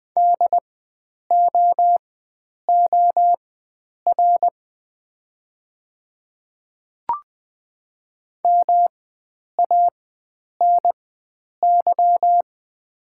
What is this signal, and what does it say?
Morse code, heard as a keyed tone, about 20 wpm, spaced as for 9 wpm, 700 Hz. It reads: DOOR MANY
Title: 4x Character Spacing - Top 300 Words - 20wpm